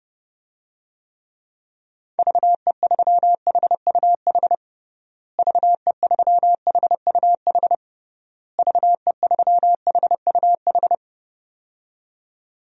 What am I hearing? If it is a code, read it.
Morse code, heard as a keyed tone, about 30 wpm, 700 Hz. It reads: VE3HUH VE3HUH VE3HUH
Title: DXCC-100 Call Signs - 6 Characters - Repeated 3x - 30wpm